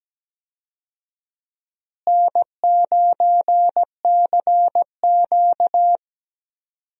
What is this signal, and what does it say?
Morse code, heard as a keyed tone, about 17 wpm, 700 Hz. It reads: N9CQ
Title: USA Call Signs - 4 Characters - Repeated 2x - 17wpm